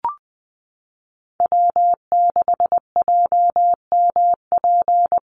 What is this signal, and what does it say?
Morse code, heard as a keyed tone, about 20 wpm, 700 Hz. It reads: W6JMP
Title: NA SOTA Chaser Callsigns 20wpm